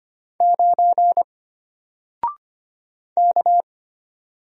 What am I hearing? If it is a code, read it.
Morse code, heard as a keyed tone, about 25 wpm, 700 Hz. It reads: 9 K
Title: Single Character (Letter:Number:Punctuation:Prosign) Alt 25wpm